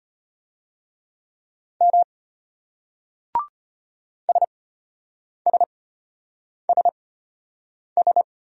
Morse code, heard as a keyed tone, about 34 wpm, spaced as for 8 wpm, 700 Hz. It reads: M SSSS